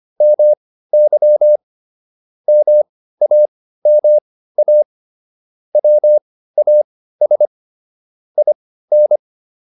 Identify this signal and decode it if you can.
Morse code, heard as a keyed tone, about 25 wpm, spaced as for 15 wpm, 600 Hz. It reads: MY MAMA WAS IN